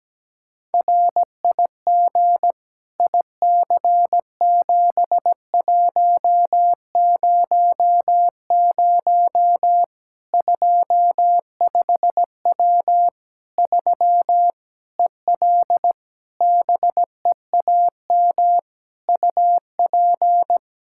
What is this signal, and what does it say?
Morse code, heard as a keyed tone, about 17 wpm, 700 Hz. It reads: RIG IC7100 25W 3 EL BEAM UP
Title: Practice QSO - Exchange 2 - Info - 17wpm